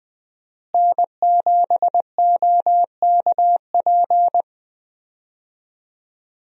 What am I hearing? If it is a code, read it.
Morse code, heard as a keyed tone, about 21 wpm, 700 Hz. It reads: N7OKP
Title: USA Call Signs - 5 Characters - Repeated 2x - 20wpm